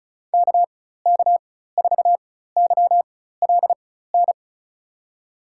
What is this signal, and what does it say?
Morse code, heard as a keyed tone, about 34 wpm, spaced as for 16 wpm, 700 Hz. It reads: KK4YLN
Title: USA Call Signs - 6 Characters - 4x Character Spacing - 35wpm